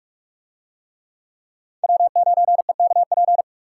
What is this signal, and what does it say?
Morse code, heard as a keyed tone, about 40 wpm, 700 Hz. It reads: W9EKP